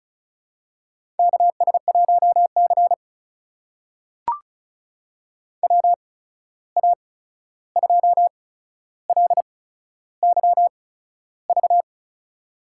Morse code, heard as a keyed tone, about 36 wpm, spaced as for 13 wpm, 700 Hz. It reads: KS1C WA2LYV
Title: USA Call Signs - 8x Character Spacing - 35wpm